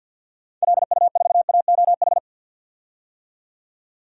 Morse code, heard as a keyed tone, about 51 wpm, 700 Hz. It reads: LU4AOH
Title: DXCC-100 Call Signs - 6 Characters - 50wpm